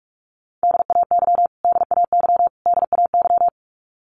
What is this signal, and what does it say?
Morse code, heard as a keyed tone, about 44 wpm, 700 Hz. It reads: DAY DAY DAY
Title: Top 200 Words Repeated 45wpm